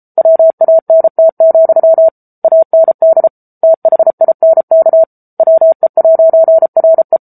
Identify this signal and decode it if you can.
Morse code, heard as a keyed tone, about 34 wpm, 650 Hz. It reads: WANT, AND THINK WE'RE
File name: Book Two / 07-Chapter 7 - 33 WPM